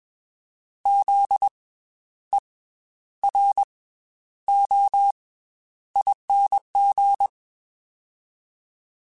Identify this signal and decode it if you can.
Morse code, heard as a keyed tone, about 22 wpm, 800 Hz. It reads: Z E R O ING